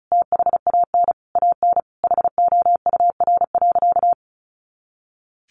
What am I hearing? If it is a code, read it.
Morse code, heard as a keyed tone, about 34 wpm, 700 Hz. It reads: THAN AN HOUR.
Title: Sentences from Top 500 Words 35wpm